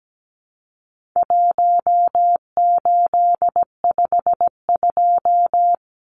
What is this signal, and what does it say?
Morse code, heard as a keyed tone, about 17 wpm, 700 Hz. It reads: 1852
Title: Numbers - 4-Digit 17wpm